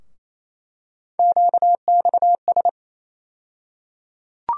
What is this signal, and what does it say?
Morse code, heard as a keyed tone, about 28 wpm, 700 Hz. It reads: QXS
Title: USA Call Sign Postfixes - 3 Letters 28wpm